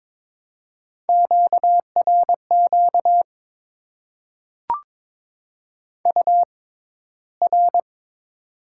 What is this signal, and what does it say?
Morse code, heard as a keyed tone, about 22 wpm, spaced as for 14 wpm, 700 Hz. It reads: QRQ UR
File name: Top 90 QSO Elements - 6x Character Spacing - 22wpm